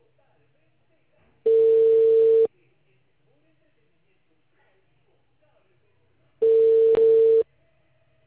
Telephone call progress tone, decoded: ringback tone

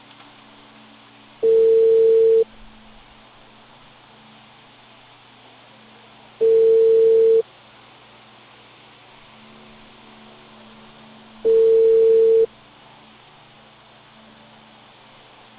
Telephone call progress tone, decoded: ringback tone

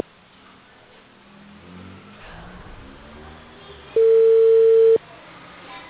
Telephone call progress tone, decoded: ringback tone

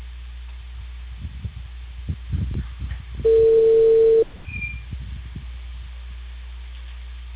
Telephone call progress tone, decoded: ringback tone

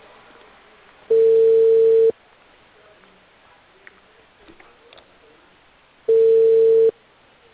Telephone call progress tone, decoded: ringback tone

